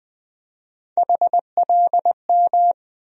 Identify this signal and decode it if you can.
Morse code, heard as a keyed tone, about 20 wpm, 700 Hz. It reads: HLM
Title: USA Call Sign Postfixes - 3 Letters 20wpm